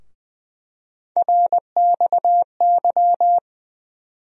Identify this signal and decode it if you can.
Morse code, heard as a keyed tone, about 21 wpm, 700 Hz. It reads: RXY
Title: USA Call Sign Postfixes - 3 Letters 20wpm